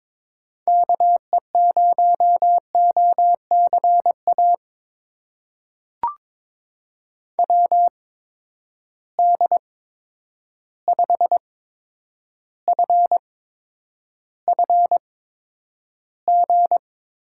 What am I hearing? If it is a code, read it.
Morse code, heard as a keyed tone, about 22 wpm, spaced as for 9 wpm, 700 Hz. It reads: KE0OCA WD5FFG
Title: USA Call Signs - 6 Characters - 8x Character Spacing - 22wpm